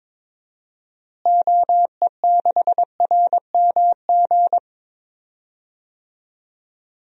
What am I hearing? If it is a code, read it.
Morse code, heard as a keyed tone, about 22 wpm, 700 Hz. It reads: OE6RMG
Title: DXCC-100 Call Signs - 6 Characters - Repeated 2x - 22wpm